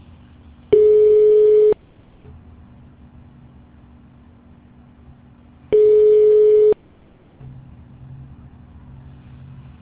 Ringback tone (Europe, 425 Hz).